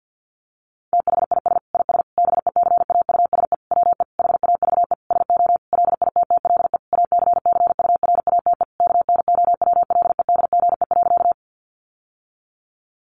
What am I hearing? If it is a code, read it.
Morse code, heard as a keyed tone, about 50 wpm, 700 Hz. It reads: THIS IS BECAUSE WE HAVE SO LITTLE ACCURATE KNOWLEDGE.
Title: Sentences from Top 1200 Words 50wpm